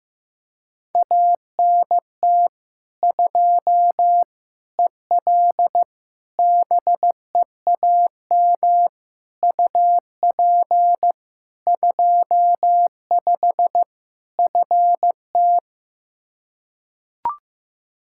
Morse code, heard as a keyed tone, about 16 wpm, 700 Hz. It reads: ANT 2 EL BEAM UP 25 FT
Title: K1USN SST QSO Practice 15wpm